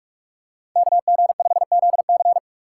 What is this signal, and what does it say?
Morse code, heard as a keyed tone, about 44 wpm, 700 Hz. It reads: KG5ZC